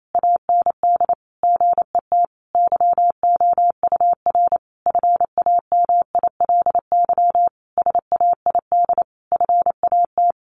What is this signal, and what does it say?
Morse code, heard as a keyed tone, about 28 wpm, 700 Hz. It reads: AND GET YOUR FAMILY SAID FAT
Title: Sentences from Top 1100 Words 28wpm